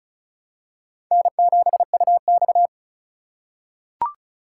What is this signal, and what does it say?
Morse code, heard as a keyed tone, about 36 wpm, 700 Hz. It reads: N7UX
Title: USA Call Signs - 2x Character Spacing - 35wpm